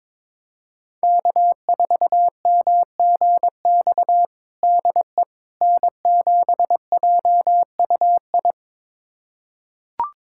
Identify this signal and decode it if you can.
Morse code, heard as a keyed tone, about 22 wpm, 700 Hz. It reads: K4MGX DE N7JUI
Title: USA Callsign DE Callsign - 5 Characters - 22wpm